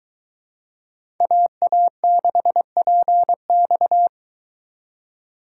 Morse code, heard as a keyed tone, about 22 wpm, 700 Hz. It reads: AA6PX